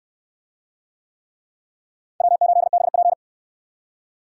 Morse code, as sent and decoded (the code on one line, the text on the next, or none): .-- ---.. -... .--.
W8BP